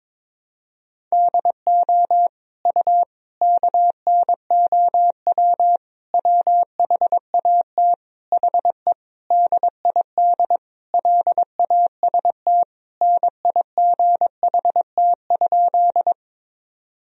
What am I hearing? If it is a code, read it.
Morse code, heard as a keyed tone, about 22 wpm, 700 Hz. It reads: DO U KNOW WHAT HE DID LAST NIGHT?
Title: Sentences from Top 300 Words and Abbreviations 22wpm